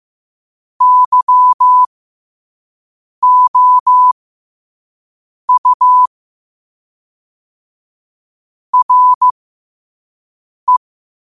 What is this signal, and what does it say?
Morse code, heard as a keyed tone, about 15 wpm, spaced as for 5 wpm, 1000 Hz. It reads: YOU RE
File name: qotdfiles / 5wpm